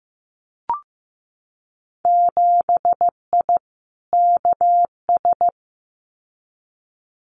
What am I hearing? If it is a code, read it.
Morse code, heard as a keyed tone, about 15 wpm, 700 Hz. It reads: 7I KS
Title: Winter Field Day Exchanges Encore 15wpm